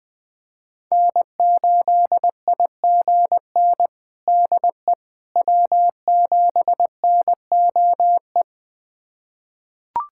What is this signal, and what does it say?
Morse code, heard as a keyed tone, about 21 wpm, 700 Hz. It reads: N8IGN DE W7NOE
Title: USA Callsign DE Callsign - 5 Characters - 20wpm